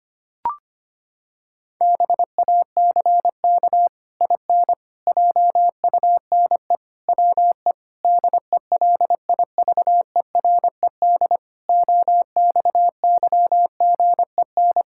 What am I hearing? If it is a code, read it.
Morse code, heard as a keyed tone, about 25 wpm, 700 Hz. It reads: BACK IN JUNE WE DELIVERED OXYGEN
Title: Pangram Challenge v2 - 25wpm